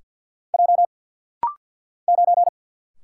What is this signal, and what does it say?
Morse code, heard as a keyed tone, about 50 wpm, 690 Hz. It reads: J 9
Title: Single Letter-Number Alt - Rapid-Fire 50wpm